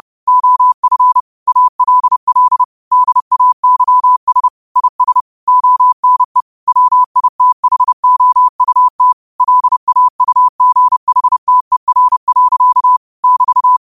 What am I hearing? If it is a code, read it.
Morse code, heard as a keyed tone, about 30 wpm, 1000 Hz. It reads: OF ALL DAYS IS ONE WITHOUT LAUGHTER. <BT>
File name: qotdfiles / 30wpm